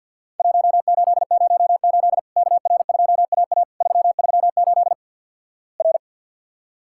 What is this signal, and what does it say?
Morse code, heard as a keyed tone, about 50 wpm, 700 Hz. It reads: 1808 KR2AA 338 R
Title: Worked All Europe DX Contest - QTC Exchange with Repeats - 50wpm